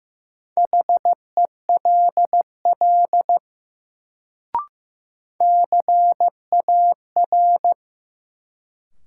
Morse code, heard as a keyed tone, about 15 wpm, 700 Hz. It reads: HELL CAR
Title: Sets of 5 Words Encore - New Word Drill - 15wpm